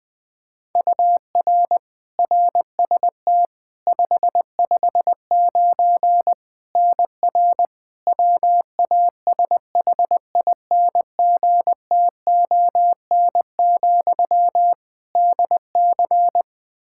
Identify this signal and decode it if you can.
Morse code, heard as a keyed tone, about 20 wpm, 700 Hz. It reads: UR RST 559 NR WASHINGTON, DC
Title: K1USN SST QSO Practice 20wpm